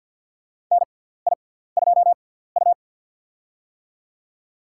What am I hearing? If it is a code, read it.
Morse code, heard as a keyed tone, about 49 wpm, spaced as for 17 wpm, 700 Hz. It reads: NI2U
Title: USA Call Signs - 4 Characters - 6x Character Spacing - 50wpm